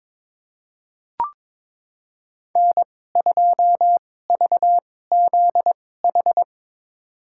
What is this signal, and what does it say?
Morse code, heard as a keyed tone, about 22 wpm, spaced as for 16 wpm, 700 Hz. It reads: N2VZH
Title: USA Call Signs - 5 Characters - 2x Character Spacing - 22wpm